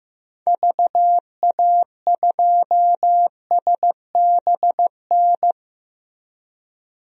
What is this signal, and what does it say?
Morse code, heard as a keyed tone, about 15 wpm, 700 Hz. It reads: VA2SBN
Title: DXCC-100 Call Signs - 6 Characters - Repeated 2x - 15wpm